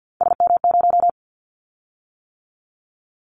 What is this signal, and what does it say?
Morse code, heard as a keyed tone, about 50 wpm, 700 Hz. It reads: SM0
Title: Callsign Prefixes for Top 25 DXCC - 50wpm